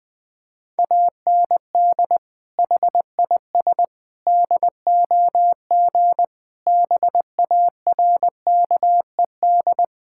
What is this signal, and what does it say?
Morse code, heard as a keyed tone, about 20 wpm, 700 Hz. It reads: AND HIS DOG BARKED
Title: Sentences with 2-Word Phrases - 20wpm